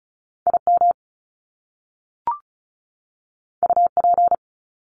Morse code, heard as a keyed tone, about 36 wpm, 700 Hz. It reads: IM UP
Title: Top Two Letter Combinations in Words 35wpm